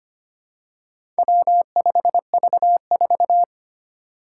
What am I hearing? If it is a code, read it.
Morse code, heard as a keyed tone, about 25 wpm, 700 Hz. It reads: W5V4
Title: Four Challenging Letter-Number Combinations 25wpm